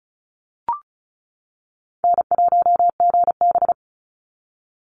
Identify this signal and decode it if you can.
Morse code, heard as a keyed tone, about 34 wpm, 700 Hz. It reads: N1GB
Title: NA SOTA Chaser Callsigns 35wpm